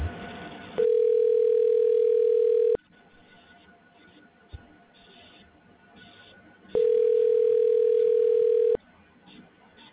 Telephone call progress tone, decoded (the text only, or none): ringback tone